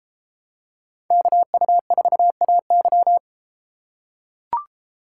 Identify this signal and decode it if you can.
Morse code, heard as a keyed tone, about 32 wpm, 700 Hz. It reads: KU4AY